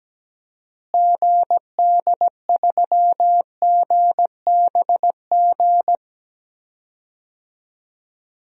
Morse code, heard as a keyed tone, about 17 wpm, 700 Hz. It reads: GD3GBG
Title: DXCC-100 Call Signs - 6 Characters - Repeated 3x - 17wpm